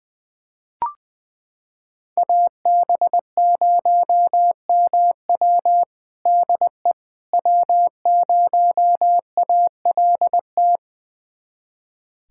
Morse code, heard as a keyed tone, about 20 wpm, 700 Hz. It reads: AB0MW DE W0ALT
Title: USA Callsign DE Callsign - 5 Characters - 20wpm